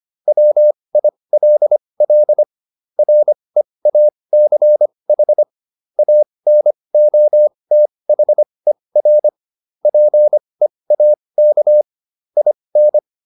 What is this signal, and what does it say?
Morse code, heard as a keyed tone, about 25 wpm, spaced as for 20 wpm, 600 Hz. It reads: WILL REACH ANOTHER PEAK IN